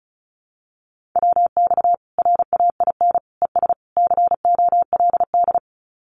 Morse code, heard as a keyed tone, about 34 wpm, 700 Hz. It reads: WX RAIN ES COLD
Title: Common QSO Phrases 35wpm